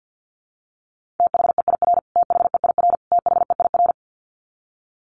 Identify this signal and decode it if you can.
Morse code, heard as a keyed tone, about 49 wpm, 700 Hz. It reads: THEIR THEIR THEIR
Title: Top 500 Words Repeated 50wpm